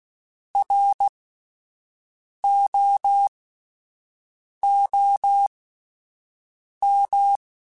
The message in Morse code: ROOM

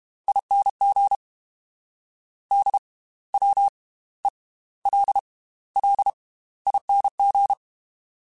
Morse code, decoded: ING DWELLING

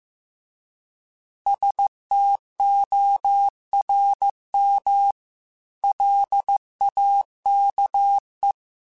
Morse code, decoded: STORM LAKE

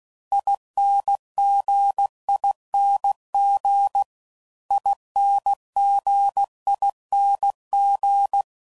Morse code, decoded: INGING INGING

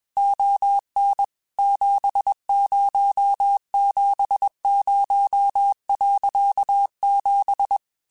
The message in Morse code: ON 7070.7

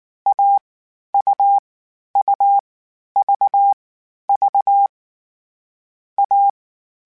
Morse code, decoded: A U U V V A

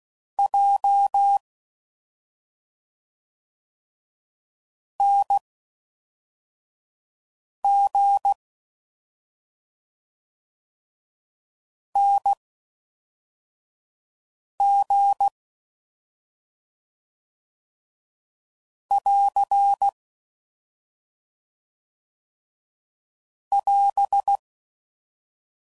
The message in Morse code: J N G N G <AR> <AS>

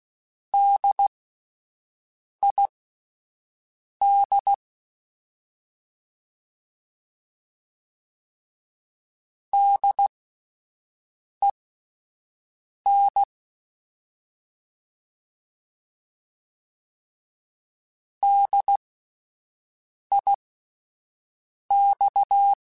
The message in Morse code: DID DEN DIX

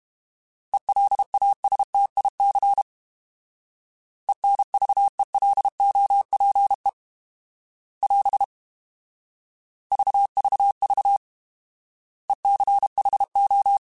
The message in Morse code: ELASTIC ENVELOPE <AS> VVV ECHO

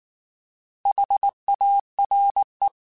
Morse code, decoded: HARE

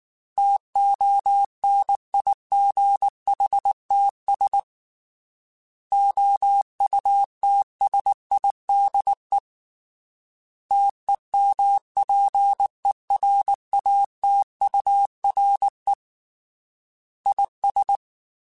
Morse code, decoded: TONIGHTS OUTSIDE TEMPERATURE IS

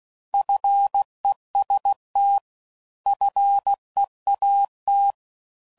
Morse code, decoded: FEST FEAT

